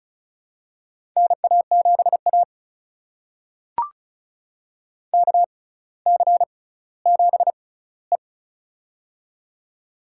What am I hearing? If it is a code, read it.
Morse code, heard as a keyed tone, about 36 wpm, spaced as for 18 wpm, 700 Hz. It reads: NA7A KC7E
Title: USA Call Signs - 4 Characters - 6x Character Spacing - 35wpm